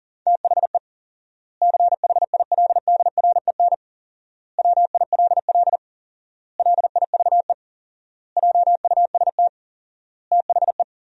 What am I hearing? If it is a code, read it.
Morse code, heard as a keyed tone, about 40 wpm, 700 Hz. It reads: THE CHILDREN WILL LIVE JUST THE
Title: 4x Word-Spacing - Sentences from Top 300 Words - 40wpm